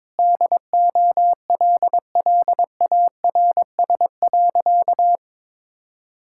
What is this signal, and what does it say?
Morse code, heard as a keyed tone, about 22 wpm, 700 Hz. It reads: DOLLARS.
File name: Sentences with 9-Word Phrases - 22wpm